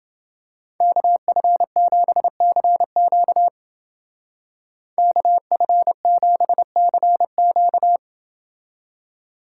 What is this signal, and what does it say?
Morse code, heard as a keyed tone, about 28 wpm, 700 Hz. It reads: KF7CQ KF7CQ